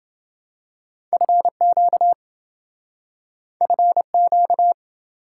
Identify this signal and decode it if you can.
Morse code, heard as a keyed tone, about 27 wpm, 700 Hz. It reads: FQ FQ